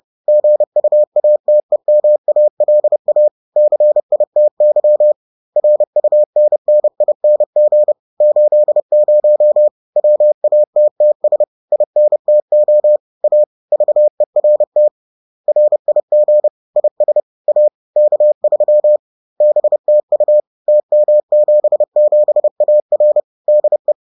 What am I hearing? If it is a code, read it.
Morse code, heard as a keyed tone, about 30 wpm, 600 Hz. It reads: GUATEMALA CITY RUNNING 80 WATTS INTO A VERT RIG IS A K3 BTU TM77AR DE